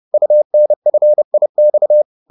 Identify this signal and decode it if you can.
Morse code, heard as a keyed tone, about 30 wpm, 600 Hz. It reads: UNFIX